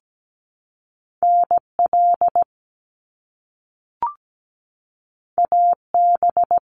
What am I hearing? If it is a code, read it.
Morse code, heard as a keyed tone, about 17 wpm, 700 Hz. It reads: NL AB
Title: Canadian Provinces and Territories 17wpm